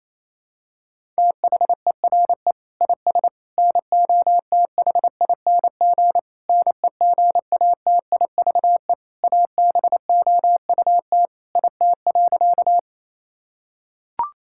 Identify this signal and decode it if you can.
Morse code, heard as a keyed tone, about 28 wpm, 700 Hz. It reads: THERE IS NOTHING NEGATIVE ABOUT IT.
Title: Sentences from Top 1000 Words - Review Part 6 - 28wpm